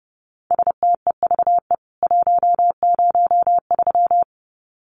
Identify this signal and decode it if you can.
Morse code, heard as a keyed tone, about 30 wpm, 700 Hz. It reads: STEVE 103
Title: CWT Exchange 30wpm